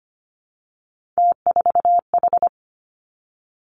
Morse code, heard as a keyed tone, about 25 wpm, 700 Hz. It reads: T4H
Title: Three Challenging Letter-Number Combinations 25wpm